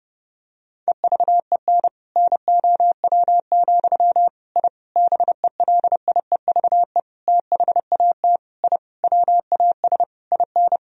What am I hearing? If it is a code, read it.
Morse code, heard as a keyed tone, about 30 wpm, 700 Hz. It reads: EVEN NOW, I BELIEVE THAT I WAS IN
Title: Sentences from Top 1000 Words - Review Part 1 - 30wpm